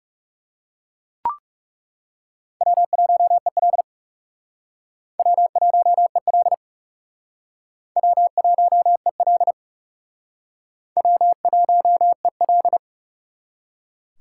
Morse code, heard as a keyed tone, about 39 wpm, 700 Hz. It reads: W1EL W1EL W1EL W1EL